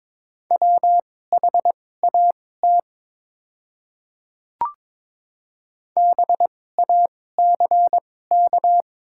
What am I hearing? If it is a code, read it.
Morse code, heard as a keyed tone, about 22 wpm, spaced as for 16 wpm, 700 Hz. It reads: WHAT BACK
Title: 2x Character Spacing - Top 200 Words - 22wpm